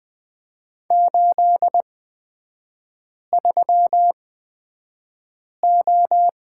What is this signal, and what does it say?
Morse code, heard as a keyed tone, about 20 wpm, spaced as for 5 wpm, 690 Hz. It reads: 83O
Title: 229 Lesson 37e - Letter x - Three Characters 20wpm